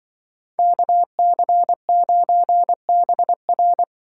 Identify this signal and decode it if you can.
Morse code, heard as a keyed tone, about 23 wpm, 700 Hz. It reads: KC9BR
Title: Speed Racing - USA 2x2 Call Signs 20wpm